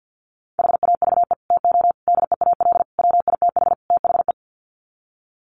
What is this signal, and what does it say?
Morse code, heard as a keyed tone, about 50 wpm, 700 Hz. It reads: HAVE TO DEAL WITH THE
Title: Sets of 5 Words Encore2 50wpm